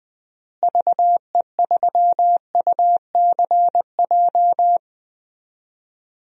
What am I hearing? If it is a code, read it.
Morse code, heard as a keyed tone, about 20 wpm, 700 Hz. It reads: VE3UCJ
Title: DXCC-100 Call Signs - 6 Characters - Repeated 3x - 20wpm